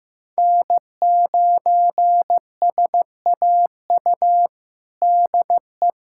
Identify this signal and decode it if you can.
Morse code, heard as a keyed tone, about 15 wpm, 700 Hz. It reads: N9SAU DE
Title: USA Callsign DE Callsign - 5 Characters - 15wpm